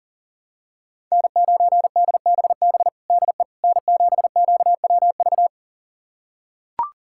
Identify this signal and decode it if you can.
Morse code, heard as a keyed tone, about 40 wpm, 690 Hz. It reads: N9DBB DE N7QWV
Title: USA Callsign DE Callsign - 5 Characters - 40wpm